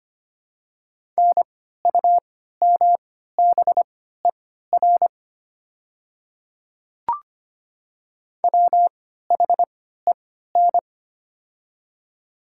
Morse code, heard as a keyed tone, about 25 wpm, spaced as for 14 wpm, 700 Hz. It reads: NUMBER WHEN
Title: 3x Character Spacing - Top 100 Words - 25wpm